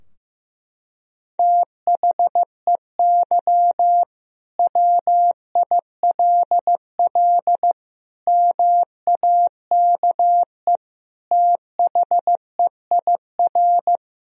Morse code, heard as a keyed tone, about 15 wpm, 700 Hz. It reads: THEY WILL MAKE THEIR
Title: Sentences from Top 1000 Words - Review Part 3 - 15wpm